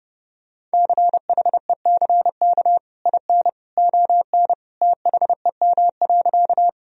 Morse code, heard as a keyed tone, about 32 wpm, 700 Hz. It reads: CHECK IN ON THEM.
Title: Sentences from Top 1000 Words - Review Part 3 - 30wpm